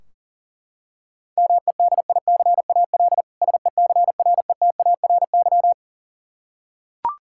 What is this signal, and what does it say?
Morse code, heard as a keyed tone, about 40 wpm, 700 Hz. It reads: MEDICAL SECRETARY
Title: K1USN SST QSO Practice 40wpm